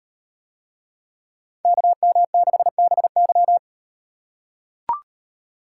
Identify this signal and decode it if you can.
Morse code, heard as a keyed tone, about 38 wpm, 700 Hz. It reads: KM6BY